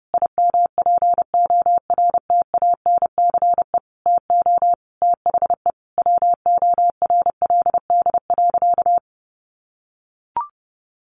Morse code, heard as a keyed tone, about 30 wpm, 700 Hz. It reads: IMPORTANCE TO THE WORLD.
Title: Sentences from Top 1200 Words 30wpm